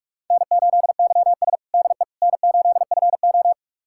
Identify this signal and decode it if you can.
Morse code, heard as a keyed tone, about 46 wpm, 700 Hz. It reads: N8YS DE N8FO
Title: USA Callsign DE Callsign - 4 Characters - 45wpm